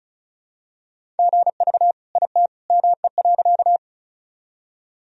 Morse code, heard as a keyed tone, about 36 wpm, 700 Hz. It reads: GV IT ME.
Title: Sentences from Top 200 Words and Abbreviations 35wpm